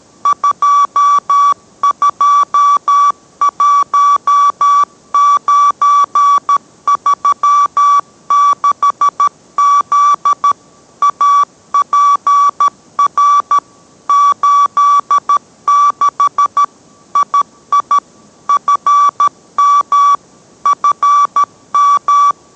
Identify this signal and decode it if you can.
Morse code, heard as a keyed tone, about 13 wpm, 1150 Hz. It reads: 221936Z APR 86 II FM FM